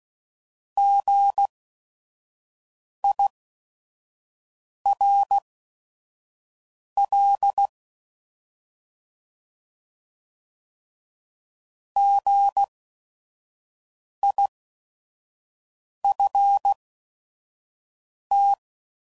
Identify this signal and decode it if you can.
Morse code, heard as a keyed tone, about 16 wpm, spaced as for 5 wpm, 800 Hz. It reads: GIRL GIFT